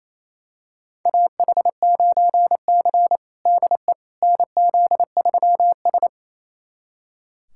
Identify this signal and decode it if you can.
Morse code, heard as a keyed tone, about 27 wpm, 700 Hz. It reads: AH9C DE NZ3S
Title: USA Callsign DE Callsign - 4 Characters - 28wpm